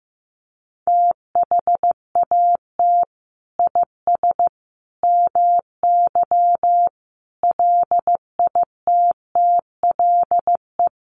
Morse code, heard as a keyed tone, about 15 wpm, 700 Hz. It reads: THAT IS MY LITTLE
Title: Sentences from Top 1500 Words 15wpm